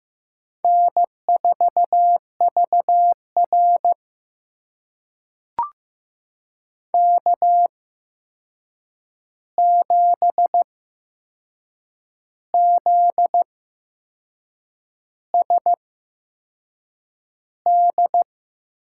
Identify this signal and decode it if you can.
Morse code, heard as a keyed tone, about 15 wpm, spaced as for 6 wpm, 700 Hz. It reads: N4VR K7ZSD
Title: USA Call Signs - 8x Character Spacing - 15wpm